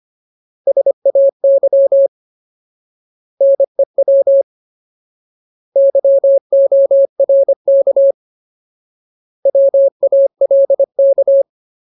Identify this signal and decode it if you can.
Morse code, heard as a keyed tone, about 25 wpm, 550 Hz. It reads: SAY NEW YORK WALK
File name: Ultimate Word List / 25WPM Ultimate